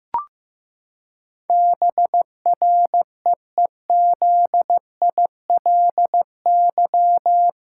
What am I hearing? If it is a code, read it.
Morse code, heard as a keyed tone, about 15 wpm, 700 Hz. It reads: BREEZILY
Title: Pangram Challenge v2 - 15wpm